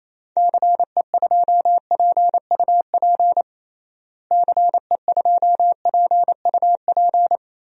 Morse code, heard as a keyed tone, about 28 wpm, 700 Hz. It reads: CE2PUP CE2PUP
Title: DXCC-100 Call Signs - 6 Characters - Repeated 3x - 28wpm